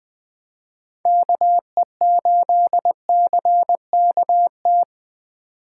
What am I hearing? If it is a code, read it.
Morse code, heard as a keyed tone, about 21 wpm, 700 Hz. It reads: KE8CKT